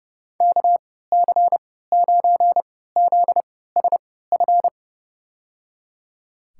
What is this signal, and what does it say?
Morse code, heard as a keyed tone, about 30 wpm, spaced as for 17 wpm, 700 Hz. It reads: KC9ZSF
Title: USA Call Signs - 6 Characters - 3x Character Spacing - 30wpm